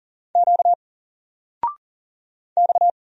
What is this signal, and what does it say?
Morse code, heard as a keyed tone, about 39 wpm, 700 Hz. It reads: Q X
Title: Single Letter-Number 40wpm